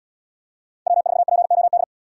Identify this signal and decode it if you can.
Morse code, heard as a keyed tone, about 71 wpm, 700 Hz. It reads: W4POD